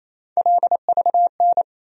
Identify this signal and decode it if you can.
Morse code, heard as a keyed tone, about 28 wpm, 700 Hz. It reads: LVN
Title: USA Call Sign Postfixes - 3 Letters 28wpm